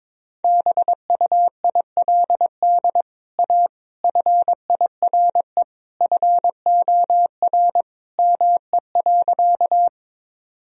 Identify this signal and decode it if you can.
Morse code, heard as a keyed tone, about 22 wpm, 700 Hz. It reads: BUILD A FIRE FOR ME.
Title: Sentences from Top 1000 Words - Review Part 1 - 22wpm